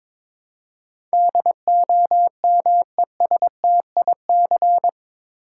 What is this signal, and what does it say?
Morse code, heard as a keyed tone, about 22 wpm, 700 Hz. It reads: DOMESTIC
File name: practice / 22wpm / Eight Letter Words - 22wpm